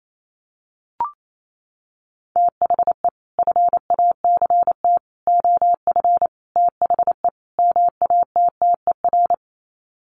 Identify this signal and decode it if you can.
Morse code, heard as a keyed tone, about 28 wpm, 700 Hz. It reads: THE FACT OF THE MATTER
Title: Sets of 5 Words 28wpm